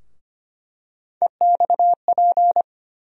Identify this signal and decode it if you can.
Morse code, heard as a keyed tone, about 25 wpm, 700 Hz. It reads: EXP
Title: USA Call Sign Postfixes - 3 Letters 25wpm